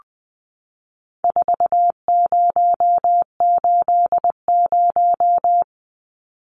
Morse code, heard as a keyed tone, about 20 wpm, 700 Hz. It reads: 4080
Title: Numbers - 4-Digit 20wpm